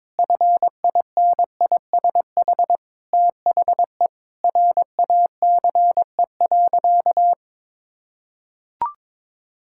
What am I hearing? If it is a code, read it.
Morse code, heard as a keyed tone, about 22 wpm, 700 Hz. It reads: FINISH THE RACE.
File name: practice / 22wpm / Sentences with 8-Word Phrases - 22wpm